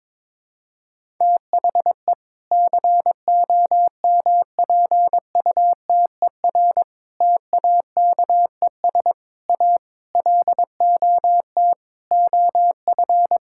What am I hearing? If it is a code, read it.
Morse code, heard as a keyed tone, about 22 wpm, 700 Hz. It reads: THE COMPUTER TAKES A LOT OF
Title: Sentences from Top 1000 Words - Review Part 6 - 22wpm